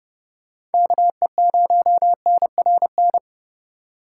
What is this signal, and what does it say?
Morse code, heard as a keyed tone, about 31 wpm, 700 Hz. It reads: KE0NRN
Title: USA Call Signs - 6 Characters - 4x Character Spacing - 30wpm